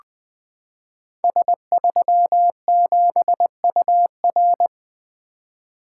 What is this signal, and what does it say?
Morse code, heard as a keyed tone, about 20 wpm, 700 Hz. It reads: S37UR
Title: Code Groups - 5 Letter:Numbers - 20wpm